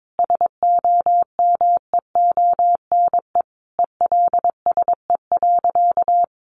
Morse code, heard as a keyed tone, about 22 wpm, 700 Hz. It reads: SOMEONE ELSE.
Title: Sentences from Top 800 Words 22wpm